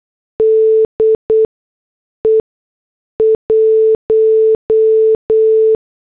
Morse code, heard as a keyed tone, about 8 wpm, spaced as for 6 wpm, 440 Hz. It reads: DE 1
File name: morse / headlines